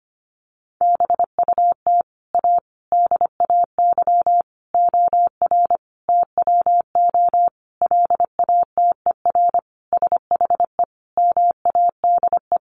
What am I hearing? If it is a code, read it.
Morse code, heard as a keyed tone, about 25 wpm, 700 Hz. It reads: BUT A DAY OR TWO LATER SHE MADE
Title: Sentences from Top 1100 Words 25wpm